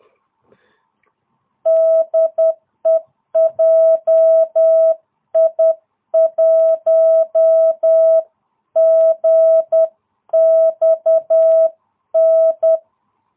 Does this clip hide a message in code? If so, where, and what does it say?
Yes: Morse code from 1.7 to 12.8 s, heard as a keyed tone, about 10 wpm, 660 Hz. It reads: DE JI1GXN